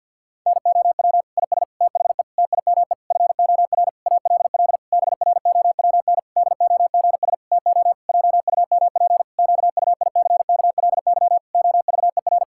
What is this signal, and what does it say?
Morse code, heard as a keyed tone, about 52 wpm, 700 Hz. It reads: NOW IS THE TIME FOR ALL BROWN DOGS TO JUMP QUICKLY OVER
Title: Pangram Challenge v2 - 50wpm